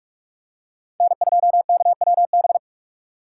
Morse code, heard as a keyed tone, about 45 wpm, 700 Hz. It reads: N2KWB
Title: USA Call Signs - 5 Characters - 4x Character Spacing - 45wpm